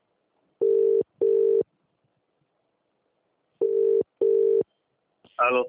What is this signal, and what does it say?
Ringback tone (UK), 400 + 450 Hz, with speech.